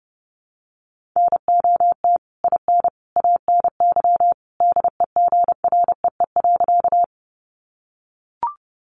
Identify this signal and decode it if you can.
Morse code, heard as a keyed tone, about 30 wpm, 700 Hz. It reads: NOT IN ANY DEGREE.
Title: Sentences from Top 1500 Words 30wpm